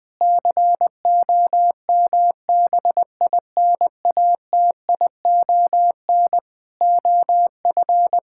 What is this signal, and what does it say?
Morse code, heard as a keyed tone, about 21 wpm, 700 Hz. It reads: COMBINATION OF
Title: Sentences from Top 1000 Words - Review Part 6 - 20wpm